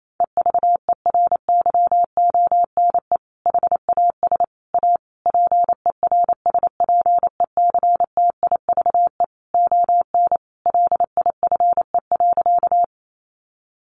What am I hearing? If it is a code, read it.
Morse code, heard as a keyed tone, about 28 wpm, 700 Hz. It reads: EVERYONE HAS A PERSPECTIVE ON LIFE.
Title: Sentences from Top 1200 Words 28wpm